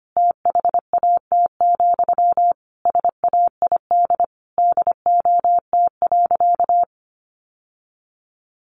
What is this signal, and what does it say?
Morse code, heard as a keyed tone, about 25 wpm, 700 Hz. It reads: THAT, SAID DOT.
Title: Sentences from Top 1500 Words 25wpm